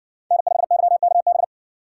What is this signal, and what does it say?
Morse code, heard as a keyed tone, about 60 wpm, 700 Hz. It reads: N5YKB